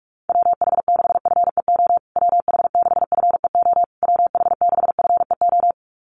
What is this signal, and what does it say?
Morse code, heard as a keyed tone, about 44 wpm, 700 Hz. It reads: WH6FEO WH6FEO WH6FEO
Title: USA Call Signs Repeated 3x 45wpm